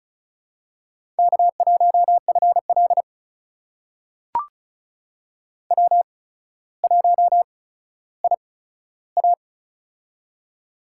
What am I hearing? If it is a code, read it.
Morse code, heard as a keyed tone, about 36 wpm, spaced as for 15 wpm, 700 Hz. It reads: K1FL W1IA
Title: USA Call Signs - 4 Characters - 8x Character Spacing - 35wpm